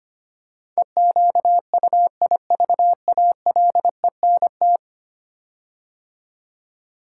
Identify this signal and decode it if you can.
Morse code, heard as a keyed tone, about 25 wpm, 700 Hz. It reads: EQUIVALENT
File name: 10 Letter Words - First 3 Letters Given - 25wpm